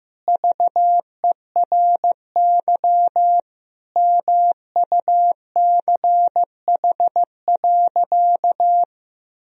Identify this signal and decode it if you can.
Morse code, heard as a keyed tone, about 15 wpm, 700 Hz. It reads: VERY MUCH.
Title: Sentences from Top 1000 Words - Review Part 1 - 15wpm